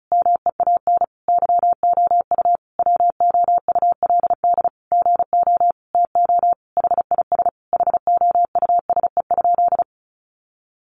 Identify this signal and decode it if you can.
Morse code, heard as a keyed tone, about 34 wpm, 700 Hz. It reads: MEAN YOU WOULD GO TO HIS HOUSE?
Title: Sentences from Top 600 Words 35wpm